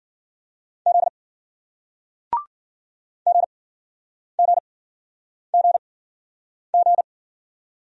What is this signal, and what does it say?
Morse code, heard as a keyed tone, about 56 wpm, spaced as for 9 wpm, 700 Hz. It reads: Z GGGG